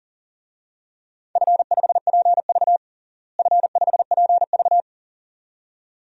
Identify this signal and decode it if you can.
Morse code, heard as a keyed tone, about 40 wpm, 700 Hz. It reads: F5PV F5PV
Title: DXCC-100 Call Signs - 4 Characters - Repeated 2x - 40wpm